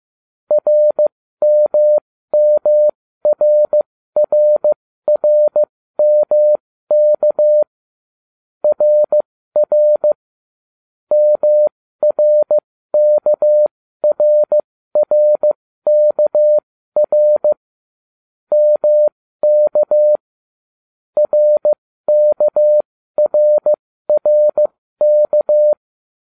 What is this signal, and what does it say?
Morse code, heard as a keyed tone, about 15 wpm, spaced as for 13 wpm, 600 Hz. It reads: RMMRRRMK RR MRKRRKR MK RKRRK